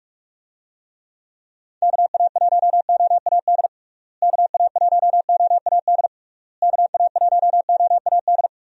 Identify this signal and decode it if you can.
Morse code, heard as a keyed tone, about 45 wpm, 700 Hz. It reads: KA1OAD KA1OAD KA1OAD
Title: USA Call Signs - 6 Characters - Repeated 3x - 45wpm